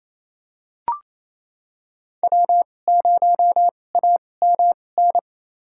Morse code, heard as a keyed tone, about 28 wpm, spaced as for 20 wpm, 700 Hz. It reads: W0AMN
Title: USA Call Signs - 5 Characters - 2x Character Spacing - 28wpm